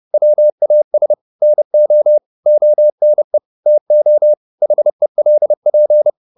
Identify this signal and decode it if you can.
Morse code, heard as a keyed tone, about 30 wpm, 600 Hz. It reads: WAS NO ONE TO HELP